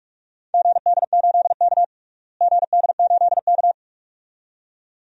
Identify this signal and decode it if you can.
Morse code, heard as a keyed tone, about 41 wpm, 700 Hz. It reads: GD8K GD8K
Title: DXCC-100 Call Signs - 4 Characters - Repeated 2x - 45wpm